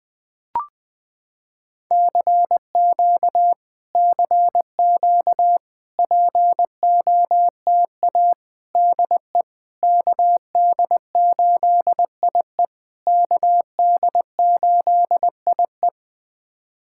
Morse code, heard as a keyed tone, about 21 wpm, 700 Hz. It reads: CQ CQ POTA DE KD8IE KD8IE
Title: POTA QSO Practice - 20wpm